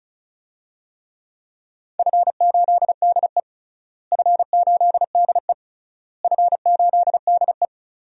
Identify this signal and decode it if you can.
Morse code, heard as a keyed tone, about 34 wpm, 700 Hz. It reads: F8DE F8DE F8DE
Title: DXCC-100 Call Signs - 4 Characters - Repeated 3x - 35wpm